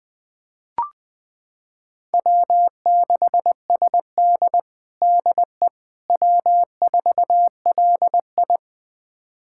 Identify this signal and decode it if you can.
Morse code, heard as a keyed tone, about 20 wpm, 700 Hz. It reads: W6SD DE W4LI
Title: USA Callsign DE Callsign - 4 Characters - 20wpm